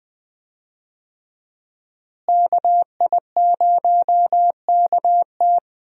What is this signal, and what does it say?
Morse code, heard as a keyed tone, about 21 wpm, 700 Hz. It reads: KI0KT
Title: USA Call Signs - 5 Characters - Repeated 3x - 20wpm